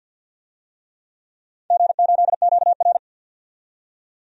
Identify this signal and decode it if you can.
Morse code, heard as a keyed tone, about 52 wpm, 700 Hz. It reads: G8QR
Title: DXCC-100 Call Signs - 4 Characters - 50wpm